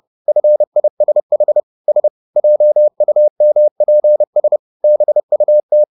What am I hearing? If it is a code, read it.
Morse code, heard as a keyed tone, about 30 wpm, 600 Hz. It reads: FISH S JUMPS BUT